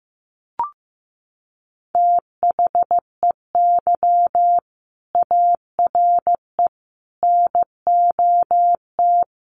Morse code, heard as a keyed tone, about 15 wpm, 700 Hz. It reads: THEY ARE NOT
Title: Sets of 5 Words Encore2 15wpm